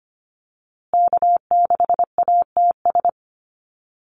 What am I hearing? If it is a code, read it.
Morse code, heard as a keyed tone, about 25 wpm, 700 Hz. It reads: K6ATS
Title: USA Call Signs Repeated 2x 25wpm